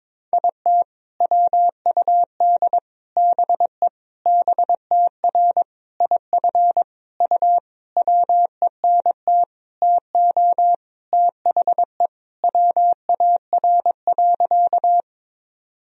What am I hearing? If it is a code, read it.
Morse code, heard as a keyed tone, about 22 wpm, 700 Hz. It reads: IT WUD BE BTR IF U WENT TO THE WAR.
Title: Sentences from Top 400 Words and Abbreviations 22wpm